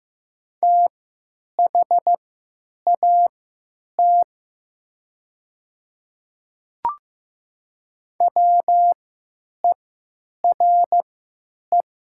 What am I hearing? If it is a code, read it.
Morse code, heard as a keyed tone, about 15 wpm, spaced as for 9 wpm, 690 Hz. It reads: THAT WERE
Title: 3x Character Spacing - Top 100 Words - 15wpm